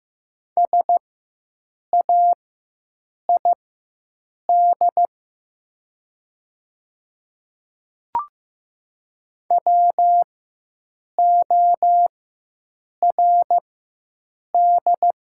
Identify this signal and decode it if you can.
Morse code, heard as a keyed tone, about 15 wpm, spaced as for 7 wpm, 700 Hz. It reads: SAID WORD
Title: 4x Character Spacing - Top 200 Words - 15wpm